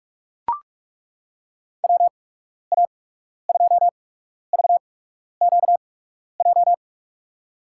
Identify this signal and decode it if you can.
Morse code, heard as a keyed tone, about 42 wpm, spaced as for 12 wpm, 700 Hz. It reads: WA2VQJ
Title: USA Call Signs - 8x Character Spacing - 45wpm